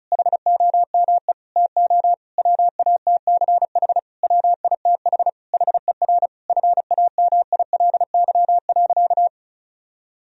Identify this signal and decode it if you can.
Morse code, heard as a keyed tone, about 34 wpm, 700 Hz. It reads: HOME TO WATCH WITH HER FAMILY.
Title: Sentences from Top 1000 Words - Review Part 4 - 35wpm